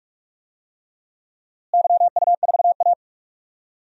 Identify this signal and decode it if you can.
Morse code, heard as a keyed tone, about 46 wpm, 700 Hz. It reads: YU4A
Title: DXCC-100 Call Signs - 4 Characters - Repeated 3x - 45wpm